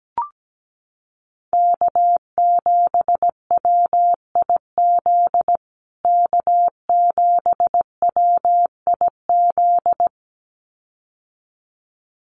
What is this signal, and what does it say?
Morse code, heard as a keyed tone, about 17 wpm, 700 Hz. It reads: K7WIZ K7WIZ
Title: USA Call Signs Repeated 2x 17wpm